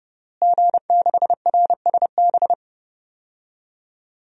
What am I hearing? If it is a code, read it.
Morse code, heard as a keyed tone, about 32 wpm, 700 Hz. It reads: G6RSB
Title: DXCC-100 Call Signs - 5 Characters - Repeated 3x - 30wpm